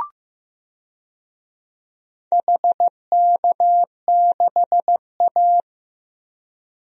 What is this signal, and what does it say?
Morse code, heard as a keyed tone, about 16 wpm, 700 Hz. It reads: HK6A
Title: DXCC-100 Call Signs - 4 Characters - Repeated 2x - 15wpm